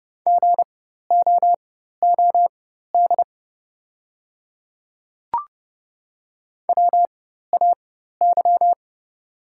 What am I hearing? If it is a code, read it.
Morse code, heard as a keyed tone, about 30 wpm, spaced as for 14 wpm, 700 Hz. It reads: GOOD WAY
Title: 4x Character Spacing - Top 200 Words - 30wpm